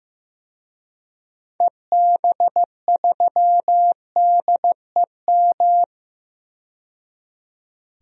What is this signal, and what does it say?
Morse code, heard as a keyed tone, about 15 wpm, 700 Hz. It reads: EB3DEM
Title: DXCC-100 Call Signs - 6 Characters - 15wpm